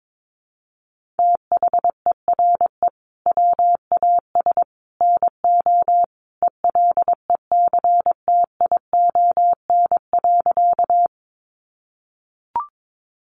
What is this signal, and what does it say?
Morse code, heard as a keyed tone, about 22 wpm, 700 Hz. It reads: THERE WAS NO ELECTION.
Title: Sentences from Top 1500 Words 22wpm